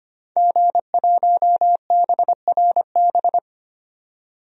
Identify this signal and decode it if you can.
Morse code, heard as a keyed tone, about 25 wpm, 700 Hz. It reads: G1BRB
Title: DXCC-100 Call Signs - 5 Characters - 25wpm